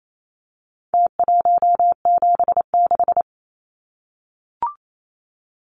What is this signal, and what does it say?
Morse code, heard as a keyed tone, about 28 wpm, 700 Hz. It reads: T176
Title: Numbers - Contest Style - 4-Digit 28wpm